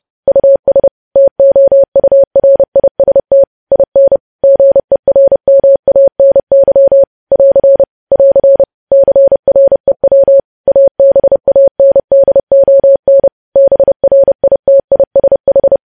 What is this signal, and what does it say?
Morse code, heard as a keyed tone, about 30 wpm, 550 Hz. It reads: US TOURIST IN GERMANY <AR> <AR> CREW ABANDON BRITISH
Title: News Headlines at 30